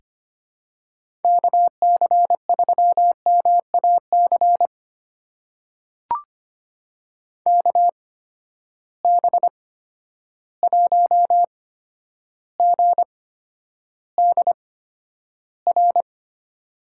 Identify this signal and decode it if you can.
Morse code, heard as a keyed tone, about 25 wpm, spaced as for 11 wpm, 700 Hz. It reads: KC3MAC KB1GDR
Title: USA Call Signs - 8x Character Spacing - 25wpm